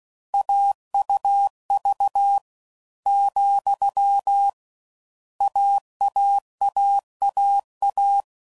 Morse code, heard as a keyed tone, about 16 wpm, 800 Hz. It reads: AUV , AAAAA
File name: Combinations / Difficult Combos / AUV AUV4